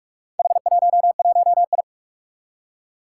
Morse code, heard as a keyed tone, about 44 wpm, 700 Hz. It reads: H11I
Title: Four Challenging Letter-Number Combinations 45wpm